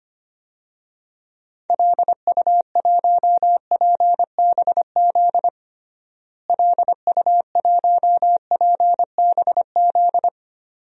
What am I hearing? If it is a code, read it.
Morse code, heard as a keyed tone, about 25 wpm, 700 Hz. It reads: LU1PBZ LU1PBZ
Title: DXCC-100 Call Signs - 6 Characters - Repeated 2x - 25wpm